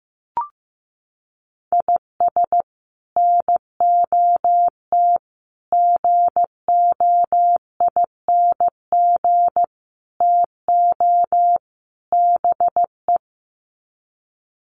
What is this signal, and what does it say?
Morse code, heard as a keyed tone, about 15 wpm, 700 Hz. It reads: IS NOT GOING TO BE
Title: Sets of 5 Words Encore2 15wpm